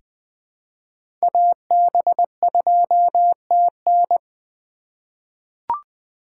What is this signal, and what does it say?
Morse code, heard as a keyed tone, about 20 wpm, 700 Hz. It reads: AB2TN
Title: USA Call Signs - 5 Characters - 3x Character Spacing - 20wpm